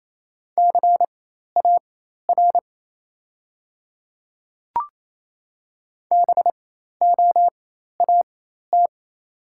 Morse code, heard as a keyed tone, about 28 wpm, spaced as for 13 wpm, 700 Hz. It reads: CAR BOAT
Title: 4x Character Spacing - Top 500 Words - 28wpm